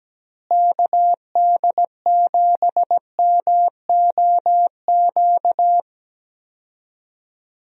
Morse code, heard as a keyed tone, about 17 wpm, 700 Hz. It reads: KD7MOQ
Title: USA Call Signs - 6 Characters - Repeated 3x - 17wpm